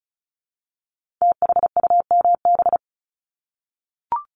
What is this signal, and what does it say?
Morse code, heard as a keyed tone, about 36 wpm, 700 Hz. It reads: THUMB
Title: Multi-letter Phonemes Repeated 3x 35wpm